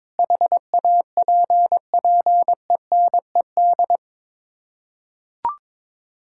Morse code, heard as a keyed tone, about 22 wpm, 700 Hz. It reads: HAPPENED
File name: practice / 22wpm / Sets of 8 Words - 22wpm